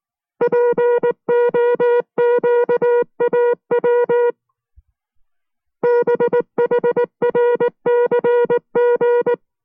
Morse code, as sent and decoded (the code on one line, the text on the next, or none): .--. --- --.- .- .-- / -... .... .-. -.-. --.
POQAW BHRCG